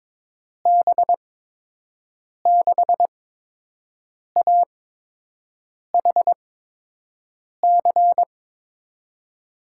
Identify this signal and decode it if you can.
Morse code, heard as a keyed tone, about 22 wpm, 700 Hz. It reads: B 6 A H C
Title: USA Call Signs - 6 Characters - 8x Character Spacing - 22wpm